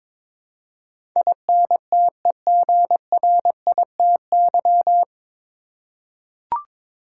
Morse code, heard as a keyed tone, about 21 wpm, 700 Hz. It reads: INTEGRITY
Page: Morse Code Ninja practice 22wpm